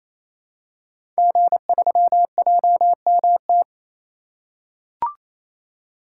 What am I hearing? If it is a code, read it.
Morse code, heard as a keyed tone, about 27 wpm, 700 Hz. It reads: G3JMT
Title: Code Groups - 5 Letter:Numbers - 28wpm